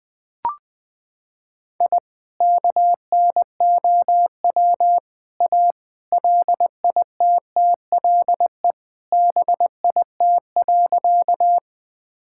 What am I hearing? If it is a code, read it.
Morse code, heard as a keyed tone, about 20 wpm, 700 Hz. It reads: I KNOW A LITTLE BIT.
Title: Sentences with 3-Word Phrases - 20wpm